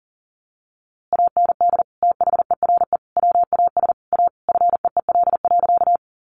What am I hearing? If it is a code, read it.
Morse code, heard as a keyed tone, about 40 wpm, 700 Hz. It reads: AND THERE WAS A FEEL.
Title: A Sound Of Thunder 40wpm